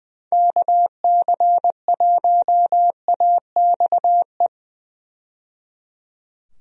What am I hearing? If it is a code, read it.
Morse code, heard as a keyed tone, about 20 wpm, 700 Hz. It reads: KC1AXE
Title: USA Call Signs - 6 Characters - 20wpm